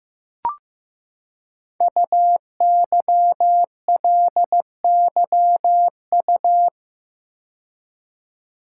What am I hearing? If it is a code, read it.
Morse code, heard as a keyed tone, about 15 wpm, 700 Hz. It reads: UYLYU
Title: Code Groups - 5 Letters - 15wpm